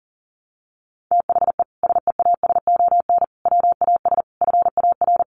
Morse code, heard as a keyed tone, about 40 wpm, 700 Hz. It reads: THE SEASON WAS FAR
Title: Sentences from Top 1400 Words 40wpm